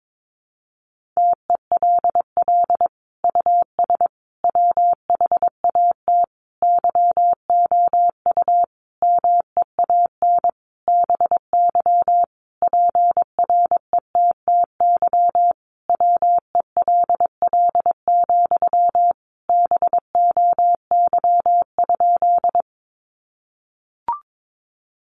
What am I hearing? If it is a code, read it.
Morse code, heard as a keyed tone, about 22 wpm, 700 Hz. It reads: TELL US WHAT YOU MEAN BY PRETTY WELL, BOY?
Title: Sentences from Top 1000 Words 22wpm